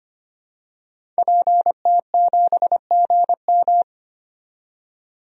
Morse code, heard as a keyed tone, about 25 wpm, 700 Hz. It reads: PT7GM
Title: DXCC-100 Call Signs - 5 Characters - 25wpm